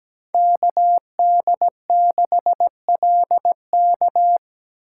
Morse code, heard as a keyed tone, about 17 wpm, 700 Hz. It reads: KD6LK
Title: USA Call Signs - 2x Character Spacing - 17wpm